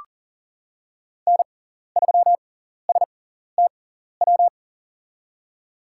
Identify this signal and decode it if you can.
Morse code, heard as a keyed tone, about 40 wpm, spaced as for 14 wpm, 700 Hz. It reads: N3STW
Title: USA Call Signs - 5 Characters - 6x Character Spacing - 40wpm